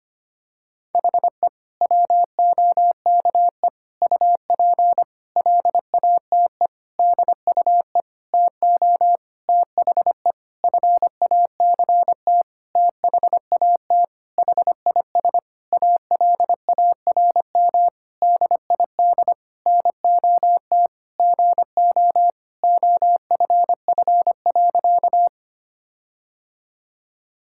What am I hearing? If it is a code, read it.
Morse code, heard as a keyed tone, about 25 wpm, 700 Hz. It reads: HE WOKE UP LATE DUE TO THE FACT THAT HIS ALARM DID NOT GO OFF.
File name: practice / 25wpm / Sentences with 4-Word Phrases - 25wpm